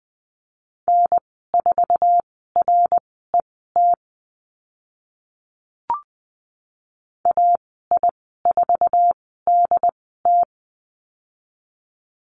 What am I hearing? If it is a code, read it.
Morse code, heard as a keyed tone, about 20 wpm, spaced as for 14 wpm, 700 Hz. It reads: N4RET AI4DT
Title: 069 Lesson 12g - Letter d - Call Signs 20wpm